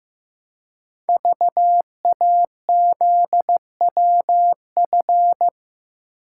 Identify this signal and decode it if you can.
Morse code, heard as a keyed tone, about 15 wpm, 700 Hz. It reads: VAZWF
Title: Code Groups - 5 Letter:Numbers - 15wpm